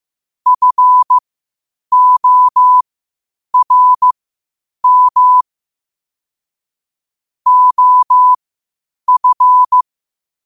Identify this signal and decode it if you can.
Morse code, heard as a keyed tone, about 15 wpm, spaced as for 8 wpm, 1000 Hz. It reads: FORM OF
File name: qotdfiles / 8wpm